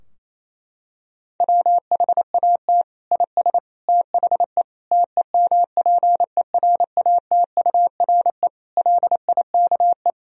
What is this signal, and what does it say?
Morse code, heard as a keyed tone, about 28 wpm, 700 Hz. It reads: WHAT IS THE TEMPERATURE LIKE
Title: Sentences from Top 1000 Words - Review Part 4 - 28wpm